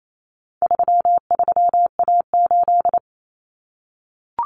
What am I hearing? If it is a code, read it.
Morse code, heard as a keyed tone, about 28 wpm, 700 Hz. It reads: 33A8
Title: Common Cut Numbers - 4-Digit 28wpm